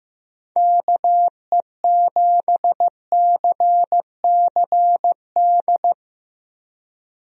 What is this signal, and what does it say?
Morse code, heard as a keyed tone, about 15 wpm, 700 Hz. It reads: KE7CCD
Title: USA Call Signs - 6 Characters - Repeated 2x - 15wpm